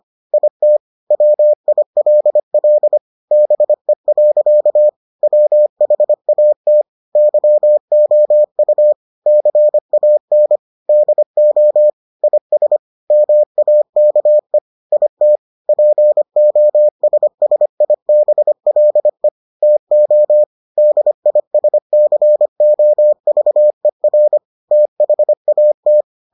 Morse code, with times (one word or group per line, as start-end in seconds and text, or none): 0.3-0.8 IT
1.1-3.0 WILL
3.3-4.9 BE.
5.2-6.8 WHAT
7.2-8.9 YOU
9.3-10.6 CAN
10.9-11.9 DO
12.2-12.8 IS
13.1-14.6 MAKE
14.9-15.4 IT
15.7-19.3 POSSIBLE
19.6-20.4 TO
20.8-24.4 DISCOVER
24.7-26.0 THAT